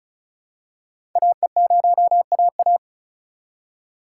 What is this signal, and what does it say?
Morse code, heard as a keyed tone, about 36 wpm, 700 Hz. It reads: AE0AA
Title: USA Call Signs - 5 Characters - 2x Character Spacing - 35wpm